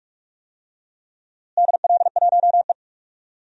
Morse code, heard as a keyed tone, about 46 wpm, 700 Hz. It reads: DL1E
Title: DXCC-100 Call Signs - 4 Characters - Repeated 2x - 45wpm